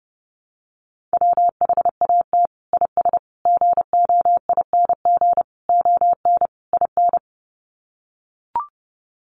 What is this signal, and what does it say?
Morse code, heard as a keyed tone, about 30 wpm, 700 Hz. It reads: WHAT IS GOING ON IN
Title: Sets of 5 Words Encore2 30wpm